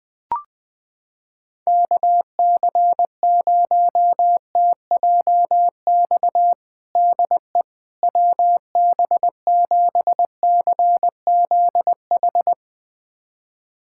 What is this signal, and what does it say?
Morse code, heard as a keyed tone, about 20 wpm, 700 Hz. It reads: KC0TJX DE WB7CZH
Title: USA Callsign DE Callsign 20wpm